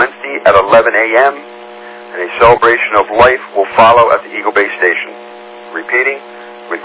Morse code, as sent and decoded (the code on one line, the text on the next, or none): none